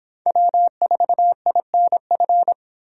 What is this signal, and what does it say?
Morse code, heard as a keyed tone, about 26 wpm, 700 Hz. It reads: W4INF